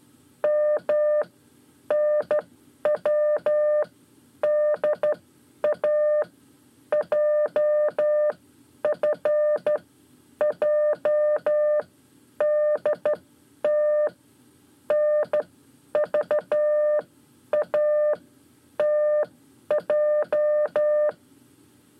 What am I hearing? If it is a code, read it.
Morse code, heard as a keyed tone, about 12 wpm, spaced as for 9 wpm, 600 Hz. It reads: MNWDA JFJDT NVATJ
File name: 5 Lezione CW 1